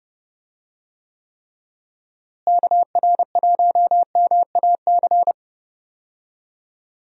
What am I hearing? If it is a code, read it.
Morse code, heard as a keyed tone, about 30 wpm, 700 Hz. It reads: KR1MAC
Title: USA Call Signs - 6 Characters - Repeated 3x - 30wpm